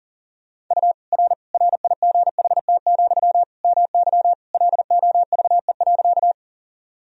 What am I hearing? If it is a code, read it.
Morse code, heard as a keyed tone, about 40 wpm, 700 Hz. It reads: U R RIGHT, MY LOVE.
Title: Sentences from Top 400 Words and Abbreviations 40wpm